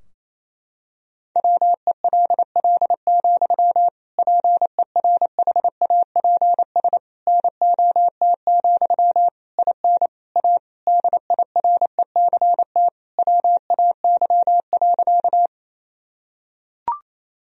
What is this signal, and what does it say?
Morse code, heard as a keyed tone, about 28 wpm, 700 Hz. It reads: WELL, PERHAPS NOT, IN A DIRECT WAY.
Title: Sentences from Top 1000 Words - Review Part 2 - 28wpm